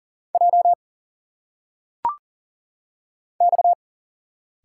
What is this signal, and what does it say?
Morse code, heard as a keyed tone, about 40 wpm, 700 Hz. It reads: J X
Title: Single Letters - Phonetic Pronunciation 40wpm